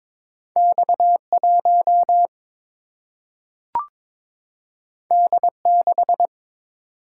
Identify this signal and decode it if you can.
Morse code, heard as a keyed tone, about 21 wpm, 700 Hz. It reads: X 1 D 6
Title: Single-Letter Single-Number 22wpm